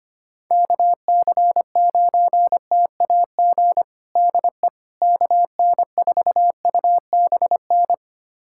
Morse code, heard as a keyed tone, about 25 wpm, 700 Hz. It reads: KC9TAG DE KN4UBN
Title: USA Callsign DE Callsign - 6 Characters - 25wpm